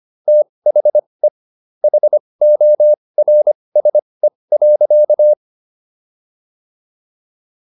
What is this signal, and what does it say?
Morse code, heard as a keyed tone, about 25 wpm, spaced as for 20 wpm, 600 Hz. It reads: THE HORSE.
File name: SS123 20